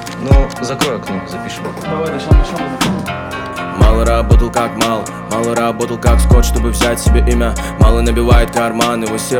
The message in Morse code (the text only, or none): none